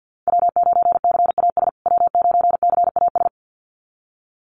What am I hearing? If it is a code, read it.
Morse code, heard as a keyed tone, about 48 wpm, 700 Hz. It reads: W9CAS W9CAS
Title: USA Call Signs Repeated 2x 50wpm